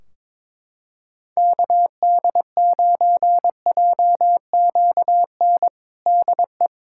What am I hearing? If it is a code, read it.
Morse code, heard as a keyed tone, about 21 wpm, 700 Hz. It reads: KD9JQN DE
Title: Practice QSO - Exchange 4 - Ending - 22wpm